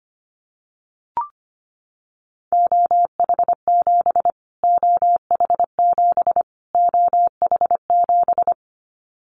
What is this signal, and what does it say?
Morse code, heard as a keyed tone, about 25 wpm, 700 Hz. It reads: OH7 OH7 OH7
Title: Callsign Prefixes for Top 25 DXCC - Repeated 3x - 25wpm